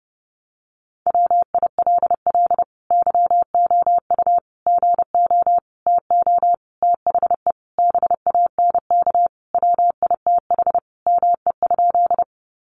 Morse code, heard as a keyed tone, about 30 wpm, 700 Hz. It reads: WILL YOU GO TO THE BANK WITH ME?
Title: Sentences from Top 900 Words 30wpm